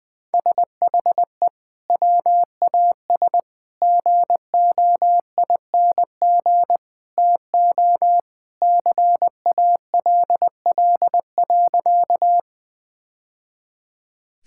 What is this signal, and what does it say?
Morse code, heard as a keyed tone, about 21 wpm, 700 Hz. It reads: SHE WAS GOING TO CALL.
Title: Sentences with 3-Word Phrases - Spoken Beginning - 20wpm